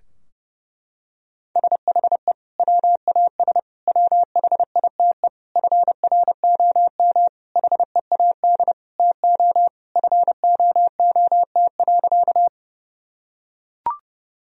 Morse code, heard as a keyed tone, about 30 wpm, 700 Hz. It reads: SHE WAS WHITE FROM HEAD TO FOOT.
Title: Sentences from Top 1000 Words - Review Part 3 - 30wpm